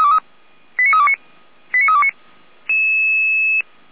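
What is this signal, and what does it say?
Five-tone selective call (ZVEI), heard as quick runs of five tones, tones of about 70 ms. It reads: 89329 89329
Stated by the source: ZVEI five-tone selective call received with an SDR.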